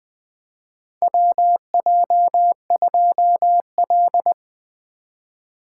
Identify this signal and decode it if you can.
Morse code, heard as a keyed tone, about 20 wpm, 700 Hz. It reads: WJ2L
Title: K1USN SST QSO Practice - 20wpm